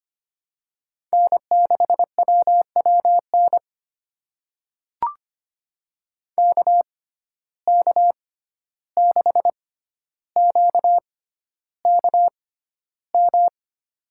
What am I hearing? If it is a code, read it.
Morse code, heard as a keyed tone, about 25 wpm, spaced as for 12 wpm, 700 Hz. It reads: N6WWN KK6QKM
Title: USA Call Signs - 6x Character Spacing - 25wpm